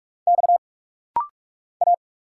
Morse code, heard as a keyed tone, about 45 wpm, 700 Hz. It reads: X A
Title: Single Letter-Number - Rapid-Fire 45wpm